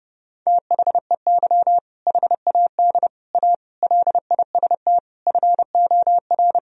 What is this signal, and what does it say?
Morse code, heard as a keyed tone, about 30 wpm, 700 Hz. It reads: THEY HAD A LIST FOR
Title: Sentences from Top 1000 Words - Review Part 2 - 30wpm